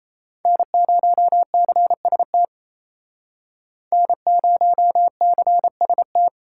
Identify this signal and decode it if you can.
Morse code, heard as a keyed tone, about 36 wpm, 700 Hz. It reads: N0CST N0CST